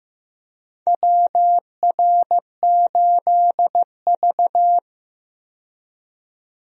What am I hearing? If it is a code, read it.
Morse code, heard as a keyed tone, about 15 wpm, 700 Hz. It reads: WR8V
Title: USA Call Signs - 4 Characters - 15wpm